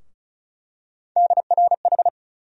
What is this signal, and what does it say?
Morse code, heard as a keyed tone, about 34 wpm, 700 Hz. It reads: DRH
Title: USA Call Sign Postfixes - 3 Letters 35wpm